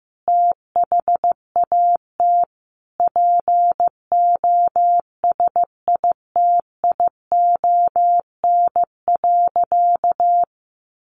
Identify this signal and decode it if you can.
Morse code, heard as a keyed tone, about 15 wpm, 700 Hz. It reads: THAT POSITION.
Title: Sentences from Top 900 Words 15wpm